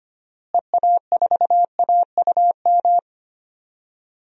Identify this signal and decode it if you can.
Morse code, heard as a keyed tone, about 25 wpm, 700 Hz. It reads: EA4AUM
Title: DXCC-100 Call Signs - 6 Characters - Repeated 3x - 25wpm